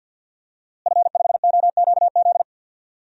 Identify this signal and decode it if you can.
Morse code, heard as a keyed tone, about 48 wpm, 700 Hz. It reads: F5OQZ
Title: DXCC-100 Call Signs - 5 Characters - Repeated 2x - 50wpm